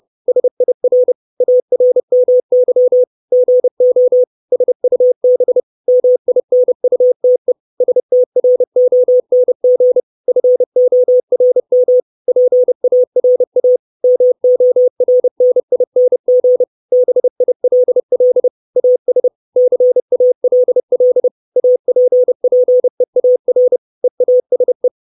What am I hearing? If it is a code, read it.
Morse code, heard as a keyed tone, about 30 wpm, 500 Hz. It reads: SIR ARMY GO SUB MINUTE STRONG FORM PARA MORNING BILL AS CALL APPEAR EASE
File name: 632 Words 30WPM